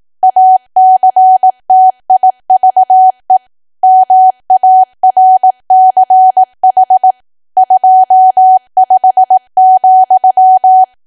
Morse code, heard as a keyed tone, about 18 wpm, 750 Hz. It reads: ACTIVE MARCH 25,
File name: cwa / W1AW-Level2 / 18WPM